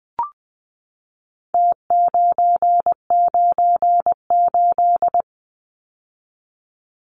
Morse code, heard as a keyed tone, about 20 wpm, 700 Hz. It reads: T998
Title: Numbers - Contest Style - 4-Digit 20wpm